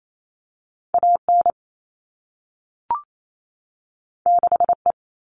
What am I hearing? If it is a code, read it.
Morse code, heard as a keyed tone, about 28 wpm, 700 Hz. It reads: AN 6E
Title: All Cut Numbers - 2-Digit 28wpm